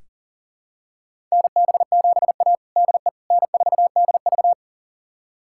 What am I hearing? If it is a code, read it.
Morse code, heard as a keyed tone, about 40 wpm, 700 Hz. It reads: NB7A DE N4DV
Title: USA Callsign DE Callsign - 4 Characters - 40wpm